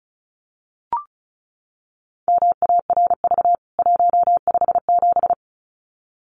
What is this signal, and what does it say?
Morse code, heard as a keyed tone, about 34 wpm, 700 Hz. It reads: MARV 157
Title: CWT Exchange 35wpm